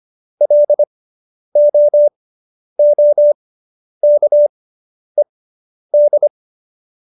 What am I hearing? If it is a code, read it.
Morse code, heard as a keyed tone, about 25 wpm, spaced as for 10 wpm, 590 Hz. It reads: LOOKED